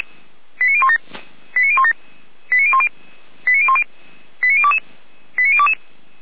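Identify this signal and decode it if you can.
Five-tone selective call (ZVEI), heard as quick runs of five tones, tones of about 70 ms. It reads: 89017 89017 89010 89010 89022 89022
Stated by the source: ZVEI five-tone selective call received by SDR.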